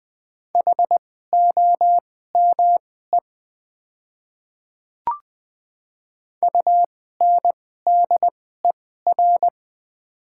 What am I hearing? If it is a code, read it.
Morse code, heard as a keyed tone, about 20 wpm, spaced as for 15 wpm, 700 Hz. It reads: HOME UNDER
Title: 2x Character Spacing - Top 200 Words - 20wpm